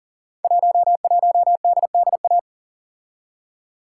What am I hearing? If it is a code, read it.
Morse code, heard as a keyed tone, about 38 wpm, 700 Hz. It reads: 11DDA
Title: Code Groups - 5 Letter:Numbers - 40wpm